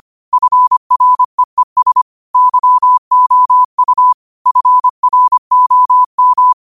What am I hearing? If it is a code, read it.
Morse code, heard as a keyed tone, about 25 wpm, 1000 Hz. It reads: FREES YOU FROM